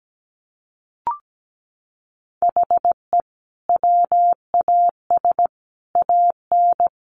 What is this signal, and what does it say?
Morse code, heard as a keyed tone, about 17 wpm, 700 Hz. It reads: HE WAS AN
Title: Sentences from Top 1200 Words 17wpm